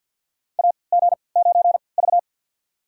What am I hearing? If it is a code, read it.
Morse code, heard as a keyed tone, about 52 wpm, spaced as for 29 wpm, 700 Hz. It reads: AG9V
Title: USA Call Signs - 4 Characters - 3x Character Spacing - 50wpm